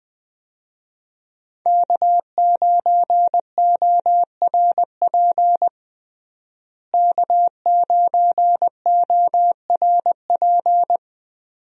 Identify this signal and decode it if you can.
Morse code, heard as a keyed tone, about 20 wpm, 700 Hz. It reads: K9ORP K9ORP
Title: USA Call Signs - 5 Characters - Repeated 2x - 20wpm